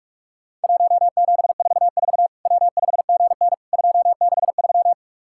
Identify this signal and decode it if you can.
Morse code, heard as a keyed tone, about 46 wpm, 700 Hz. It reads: 1744 W5GN 263
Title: Worked All Europe DX Contest - QTC Exchange - 45wpm